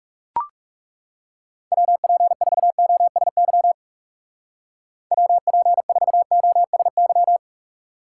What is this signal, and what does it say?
Morse code, heard as a keyed tone, about 38 wpm, 700 Hz. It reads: WP4OSY WP4OSY